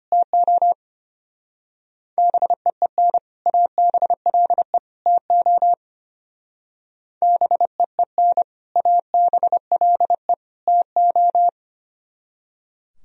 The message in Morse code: TO BEEN ABLE TO BEEN ABLE TO

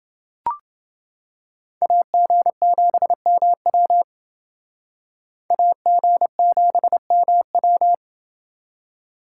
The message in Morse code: AG7MW AG7MW

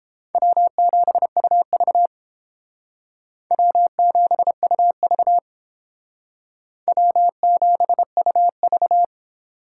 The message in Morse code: W7UV W7UV W7UV

